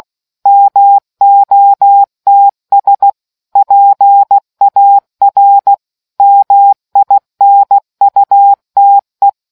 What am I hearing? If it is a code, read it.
Morse code, heard as a keyed tone, about 16 wpm, 800 Hz. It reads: MOTS PAR MINUTE